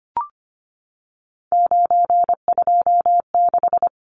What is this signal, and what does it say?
Morse code, heard as a keyed tone, about 25 wpm, 690 Hz. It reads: 926
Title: Numbers - 3-Digit 25wpm